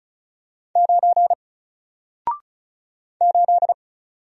Morse code, heard as a keyed tone, about 35 wpm, 700 Hz. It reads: 9 8
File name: Single-Letter-Number-35wpm